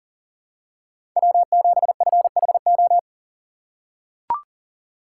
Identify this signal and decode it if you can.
Morse code, heard as a keyed tone, about 40 wpm, 700 Hz. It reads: W7FHO
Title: USA Call Signs - 4x Character Spacing - 40wpm